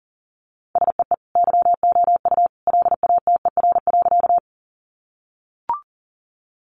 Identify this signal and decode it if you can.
Morse code, heard as a keyed tone, about 40 wpm, 700 Hz. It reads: SEE YOU LATER.
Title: Sentences from Top 700 Words 40wpm